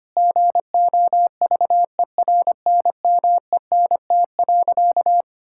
Morse code, heard as a keyed tone, about 25 wpm, 700 Hz. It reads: GOVERNMENT.
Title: Sentences from Top 1000 Words - Review Part 6 - 25wpm